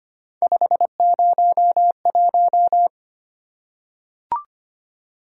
Morse code, heard as a keyed tone, about 25 wpm, 700 Hz. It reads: 501
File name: Three Letter-Number Combinations - 25wpm